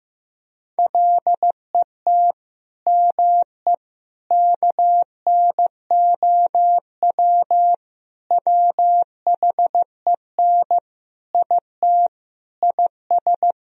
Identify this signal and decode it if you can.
Morse code, heard as a keyed tone, about 15 wpm, 700 Hz. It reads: LET ME KNOW WHEN IT IS
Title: Sentences from Top 1000 Words - Review Part 2 - 15wpm